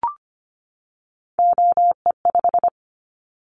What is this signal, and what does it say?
Morse code, heard as a keyed tone, about 25 wpm, 700 Hz. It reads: OE5
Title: Callsign Prefixes for Top 50 DXCC - 25wpm